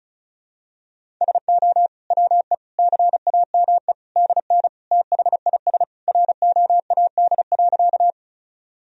Morse code, heard as a keyed tone, about 36 wpm, 700 Hz. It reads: SO WE CAME DN THIS ROAD.
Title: Sentences from Top 400 Words and Abbreviations 35wpm